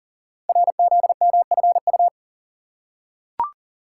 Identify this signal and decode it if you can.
Morse code, heard as a keyed tone, about 41 wpm, 690 Hz. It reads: RZMFU
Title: Code Groups - 5 Letter:Numbers - 40wpm